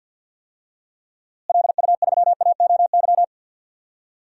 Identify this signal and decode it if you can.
Morse code, heard as a keyed tone, about 47 wpm, 700 Hz. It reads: LU3AOY
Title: DXCC-100 Call Signs - 6 Characters - Repeated 3x - 50wpm